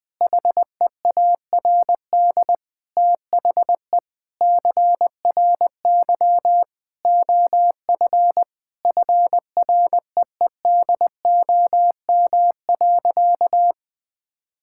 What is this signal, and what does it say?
Morse code, heard as a keyed tone, about 20 wpm, 700 Hz. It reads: HEARD THE CRY OF FREEDOM.
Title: Sentences from Top 1000 Words - Review Part 5 - 20wpm